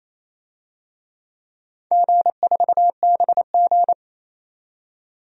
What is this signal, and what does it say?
Morse code, heard as a keyed tone, about 28 wpm, 700 Hz. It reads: G4BG
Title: DXCC-100 Call Signs - 4 Characters - 28wpm